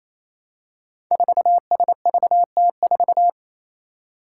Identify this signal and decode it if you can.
Morse code, heard as a keyed tone, about 28 wpm, 700 Hz. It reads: 4SVT4
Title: Five Challenging Letter-Number Combinations 28wpm